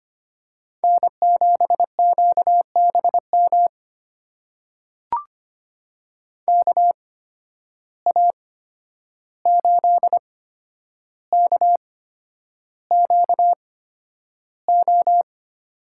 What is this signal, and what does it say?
Morse code, heard as a keyed tone, about 25 wpm, spaced as for 10 wpm, 700 Hz. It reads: N7QBM KA8KQO
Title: USA Call Signs - 8x Character Spacing - 25wpm